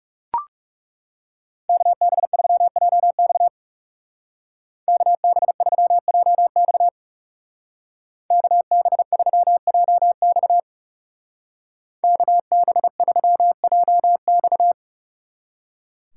Morse code, heard as a keyed tone, about 37 wpm, 700 Hz. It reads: KB3JX KB3JX KB3JX KB3JX
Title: Speed Racing - USA 2x2 Call Signs 30wpm